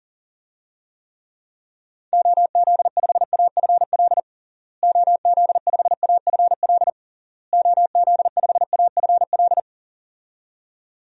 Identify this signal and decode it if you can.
Morse code, heard as a keyed tone, about 37 wpm, 700 Hz. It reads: OZ5AFL OZ5AFL OZ5AFL
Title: DXCC-100 Call Signs - 6 Characters - Repeated 3x - 40wpm